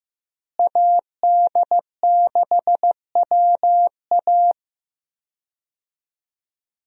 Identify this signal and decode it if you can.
Morse code, heard as a keyed tone, about 15 wpm, 700 Hz. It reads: AD6WA
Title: USA Call Signs - 5 Characters - Repeated 3x - 15wpm